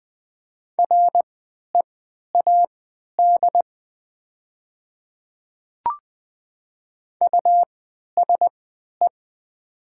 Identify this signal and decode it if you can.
Morse code, heard as a keyed tone, about 20 wpm, spaced as for 11 wpm, 700 Hz. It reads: READ USE
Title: 3x Character Spacing - Top 200 Words - 20wpm